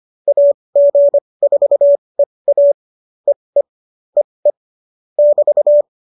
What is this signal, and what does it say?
Morse code, heard as a keyed tone, about 25 wpm, spaced as for 20 wpm, 570 Hz. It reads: AG4EA EE EE <BT>